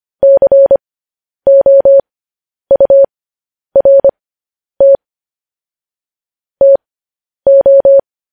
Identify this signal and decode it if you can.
Morse code, heard as a keyed tone, about 25 wpm, spaced as for 10 wpm, 550 Hz. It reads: COURT TO